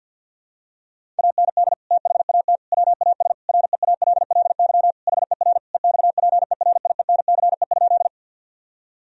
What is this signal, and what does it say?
Morse code, heard as a keyed tone, about 51 wpm, 700 Hz. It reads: AND THAT WAS REALLY HER EXPERIENCE?
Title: Sentences from Top 1000 Words - Review Part 3 - 50wpm